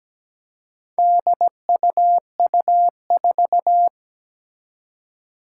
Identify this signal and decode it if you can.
Morse code, heard as a keyed tone, about 17 wpm, 700 Hz. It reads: DUU4
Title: Four Challenging Letter-Number Combinations 17wpm